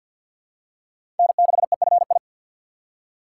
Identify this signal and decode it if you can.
Morse code, heard as a keyed tone, about 50 wpm, 700 Hz. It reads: N6EFI
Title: USA Call Signs - 5 Characters - 4x Character Spacing - 50wpm